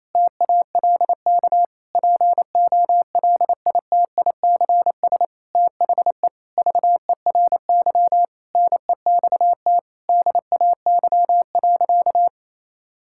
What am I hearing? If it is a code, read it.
Morse code, heard as a keyed tone, about 28 wpm, 700 Hz. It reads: TALK POLITICS THE VERY NEXT DAY.
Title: Sentences from Top 1000 Words - Review Part 6 - 28wpm